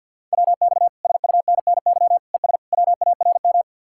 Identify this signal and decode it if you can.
Morse code, heard as a keyed tone, about 65 wpm, 700 Hz. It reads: WX SUNNY ES WARM